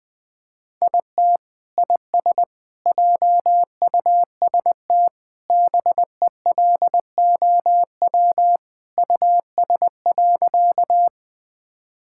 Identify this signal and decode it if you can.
Morse code, heard as a keyed tone, about 20 wpm, 700 Hz. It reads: IT IS JUST BELOW US.
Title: Sentences from Top 1000 Words - Review Part 5 - 20wpm